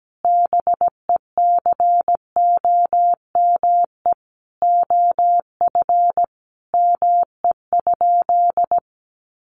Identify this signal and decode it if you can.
Morse code, heard as a keyed tone, about 17 wpm, 700 Hz. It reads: BECOME OF ME?
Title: Sentences from Top 600 Words 17wpm